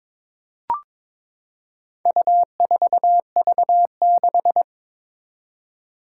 Morse code, heard as a keyed tone, about 22 wpm, 700 Hz. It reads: U4V6
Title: Four Challenging Letter-Number Combinations 22wpm